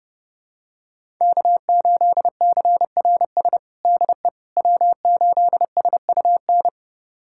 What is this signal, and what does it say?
Morse code, heard as a keyed tone, about 30 wpm, 700 Hz. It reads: K8CRS DE W8SUN
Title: USA Callsign DE Callsign - 5 Characters - 30wpm